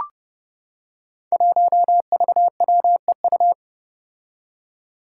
Morse code, heard as a keyed tone, about 32 wpm, 700 Hz. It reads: 1VWEU
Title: Five Challenging Letter-Number Combinations 30wpm